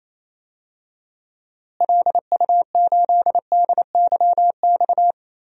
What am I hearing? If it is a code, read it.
Morse code, heard as a keyed tone, about 27 wpm, 700 Hz. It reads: LU8DYX
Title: DXCC-100 Call Signs - 6 Characters - Repeated 3x - 28wpm